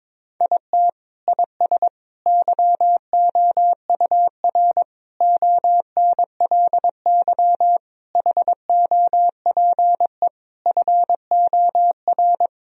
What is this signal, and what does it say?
Morse code, heard as a keyed tone, about 22 wpm, 700 Hz. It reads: IT IS YOUR ONLY HOPE FOR
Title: Sentences from Top 1000 Words - Review Part 4 - 22wpm